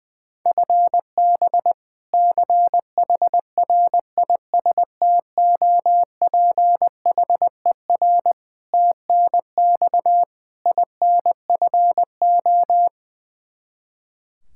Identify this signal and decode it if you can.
Morse code, heard as a keyed tone, about 20 wpm, 700 Hz. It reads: FB CHRISTOPHER TNX INFO
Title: K1USN SST QSO Practice 20wpm